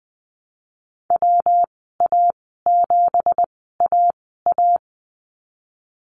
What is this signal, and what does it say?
Morse code, heard as a keyed tone, about 19 wpm, spaced as for 14 wpm, 700 Hz. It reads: WA7AA
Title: 173 Lesson 28g - Number 9 - Call Signs 20wpm